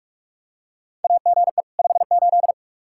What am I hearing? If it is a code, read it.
Morse code, heard as a keyed tone, about 46 wpm, 700 Hz. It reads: AGE 58
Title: Practice QSO - Exchange 3 - More Info - 45wpm